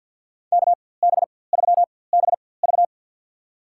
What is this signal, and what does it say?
Morse code, heard as a keyed tone, about 51 wpm, spaced as for 23 wpm, 700 Hz. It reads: KB3BV
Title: USA Call Signs - 5 Characters - 4x Character Spacing - 50wpm